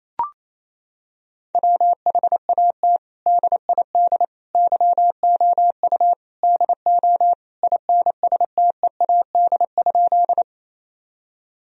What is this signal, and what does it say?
Morse code, heard as a keyed tone, about 27 wpm, 700 Hz. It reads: WHAT DID YOU DO INSTEAD?
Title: Sentences from Top 1000 Words - Review Part 4 - 28wpm